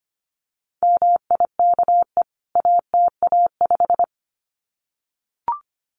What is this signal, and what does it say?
Morse code, heard as a keyed tone, about 25 wpm, 700 Hz. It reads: MIKE ATA5
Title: CWT Exchange 25wpm